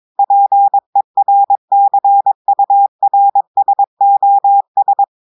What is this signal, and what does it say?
Morse code, heard as a keyed tone, about 22 wpm, 800 Hz. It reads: PERCURSOS
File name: audios / portugais / portugais texte en clair 1 vitesse 22 mots minutes